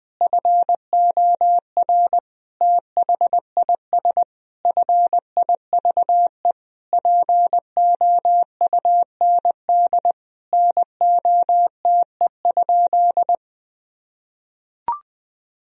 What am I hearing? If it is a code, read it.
Morse code, heard as a keyed tone, about 20 wpm, 700 Hz. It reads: FOR THIS FIVE POUND NOTE?
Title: Sentences from Top 1000 Words - Review Part 2 - 20wpm